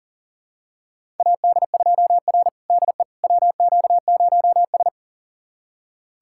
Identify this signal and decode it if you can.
Morse code, heard as a keyed tone, about 40 wpm, 690 Hz. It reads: AD2R DE WQ0S
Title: USA Callsign DE Callsign - 4 Characters - 40wpm